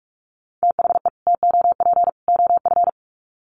Morse code, heard as a keyed tone, about 44 wpm, 700 Hz. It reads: THE TOP OF
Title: Sets of 3 Words 45wpm